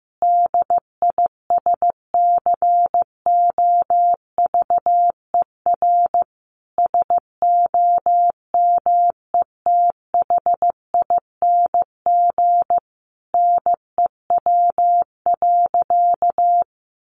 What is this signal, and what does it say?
Morse code, heard as a keyed tone, about 15 wpm, 700 Hz. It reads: DISCOVER SOMETHING NEW.
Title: Sentences from Top 1100 Words 15wpm